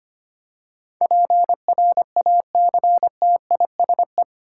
Morse code, heard as a keyed tone, about 25 wpm, 700 Hz. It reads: PRACTISE
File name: Eight Letter Words - 25wpm